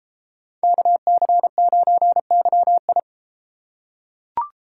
Morse code, heard as a keyed tone, about 30 wpm, 700 Hz. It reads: KC9YI